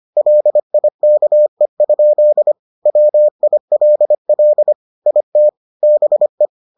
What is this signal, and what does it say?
Morse code, heard as a keyed tone, about 25 wpm, 600 Hz. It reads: LIKE? WILL IT BE